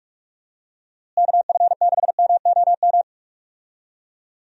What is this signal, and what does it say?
Morse code, heard as a keyed tone, about 46 wpm, 700 Hz. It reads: KF6MOM
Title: USA Call Signs - 6 Characters - Repeated 3x - 45wpm